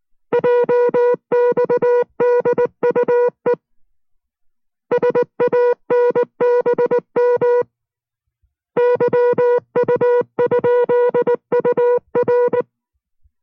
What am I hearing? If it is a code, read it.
Morse code, heard as a keyed tone, about 19 wpm, 490 Hz. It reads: JXDUE SANBM YU?UR